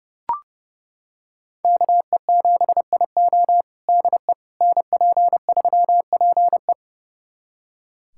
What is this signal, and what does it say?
Morse code, heard as a keyed tone, about 30 wpm, 700 Hz. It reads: KE7IO DE NP3PE
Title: USA Callsign DE Callsign - 5 Characters - 30wpm